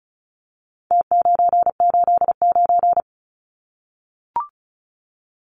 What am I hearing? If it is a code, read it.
Morse code, heard as a keyed tone, about 34 wpm, 700 Hz. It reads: T989
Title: Numbers - Contest Style - 4-Digit 35wpm